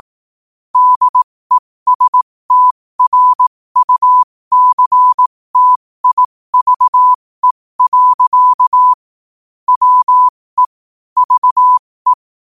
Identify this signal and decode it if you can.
Morse code, heard as a keyed tone, about 18 wpm, spaced as for 15 wpm, 1000 Hz. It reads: DESTRUCTIVE. WE VE
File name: qotdfiles / 15wpm